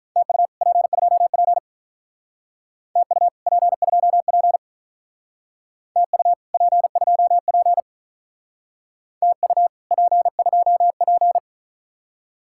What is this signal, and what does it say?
Morse code, heard as a keyed tone, about 46 wpm, 700 Hz. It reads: TU P2P TU P2P TU P2P TU P2P